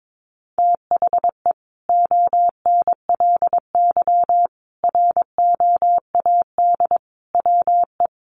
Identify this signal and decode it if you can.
Morse code, heard as a keyed tone, about 22 wpm, 700 Hz. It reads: THE ONLY ROAD WE
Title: Sentences from Top 1100 Words 22wpm